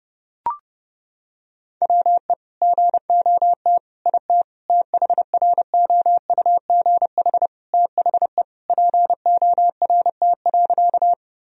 Morse code, heard as a keyed tone, about 30 wpm, 700 Hz. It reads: WE GOT IT THROUGH THE PORT.
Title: Sentences from Top 1000 Words - Review Part 1 - 30wpm